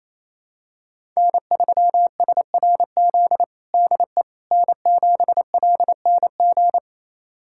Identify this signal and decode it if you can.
Morse code, heard as a keyed tone, about 28 wpm, 700 Hz. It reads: N3SRZ DE N7LNG
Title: USA Callsign DE Callsign - 5 Characters - 28wpm